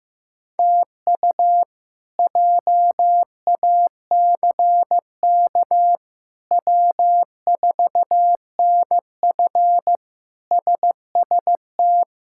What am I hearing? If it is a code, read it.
Morse code, heard as a keyed tone, about 15 wpm, 700 Hz. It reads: TU JACK W4NF SST
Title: K1USN SST QSO Practice - 15wpm